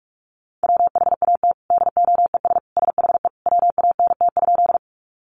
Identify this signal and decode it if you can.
Morse code, heard as a keyed tone, about 44 wpm, 700 Hz. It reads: WHAT DOES SHE WANT?
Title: Sentences from Top 200 Words 45wpm